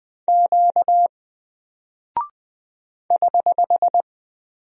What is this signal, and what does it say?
Morse code, heard as a keyed tone, about 20 wpm, 700 Hz. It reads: Q <HH>
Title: Single Character (Letter:Number:Punctuation:Prosign) Alt 20wpm